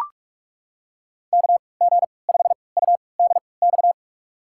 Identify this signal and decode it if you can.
Morse code, heard as a keyed tone, about 44 wpm, spaced as for 25 wpm, 700 Hz. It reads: KG5UDX
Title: USA Call Signs - 3x Character Spacing - 45wpm